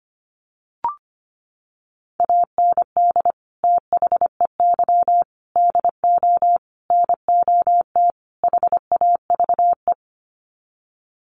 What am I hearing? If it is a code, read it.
Morse code, heard as a keyed tone, about 25 wpm, 700 Hz. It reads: AND THEY DO NOT HAVE
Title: Sets of 5 Words Encore2 25wpm